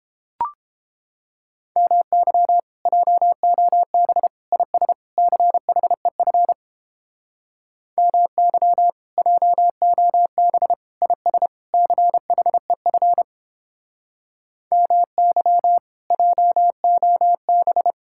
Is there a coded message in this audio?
Yes: Morse code from 1.8 to 17.9 s, heard as a keyed tone, about 29 wpm, 700 Hz. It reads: MY JOB IS CHEF MY JOB IS CHEF MY JOB